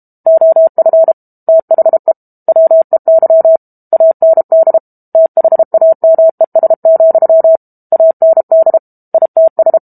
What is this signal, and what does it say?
Morse code, heard as a keyed tone, about 32 wpm, 650 Hz. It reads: OF THE WEY AND THAMES, AND ITS
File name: Book Two / 06-Chapter 6 - 32 WPM